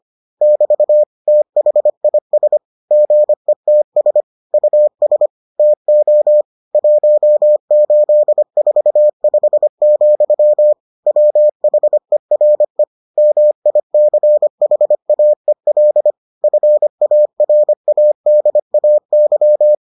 Morse code, heard as a keyed tone, about 25 wpm, 600 Hz. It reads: <BT> THIS GETS US TO 1845, WHERE MICHAEL FARADAY